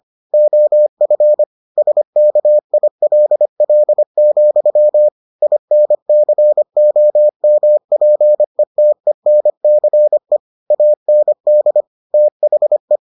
Morse code, heard as a keyed tone, about 25 wpm, 600 Hz. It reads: OF SKILL, INCOMPETENCE AND THE